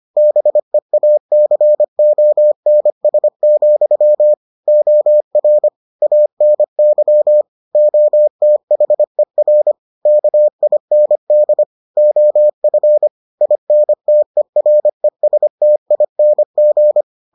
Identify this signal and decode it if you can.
Morse code, heard as a keyed tone, about 25 wpm, 600 Hz. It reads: BEACONS, OR ANY OTHER KIND OF INTERESTING